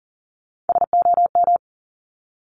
Morse code, heard as a keyed tone, about 40 wpm, 700 Hz. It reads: SOM
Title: Top Three Letter Combinations in Words 40wpm